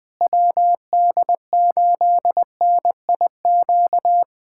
Morse code, heard as a keyed tone, about 21 wpm, 700 Hz. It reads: WD8NIQ
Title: USA Callsign DE Callsign 20wpm